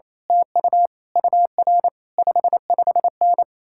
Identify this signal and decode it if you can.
Morse code, heard as a keyed tone, about 26 wpm, 700 Hz. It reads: TU UR 55N
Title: Speed-Racing Common QSO Phrases Encore - 28wpm